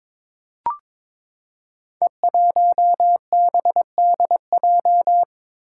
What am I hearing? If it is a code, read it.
Morse code, heard as a keyed tone, about 22 wpm, 700 Hz. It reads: E1BDJ
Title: Code Groups - 5 Letter:Numbers - 22wpm